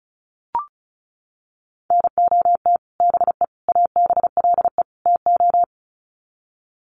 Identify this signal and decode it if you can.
Morse code, heard as a keyed tone, about 34 wpm, 700 Hz. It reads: NOT BE ABLE TO
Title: Sets of 4 Words 35wpm